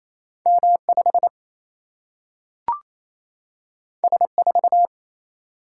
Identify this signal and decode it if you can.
Morse code, heard as a keyed tone, about 28 wpm, 700 Hz. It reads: M 5 S 4
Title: Single-Letter Single-Number 28wpm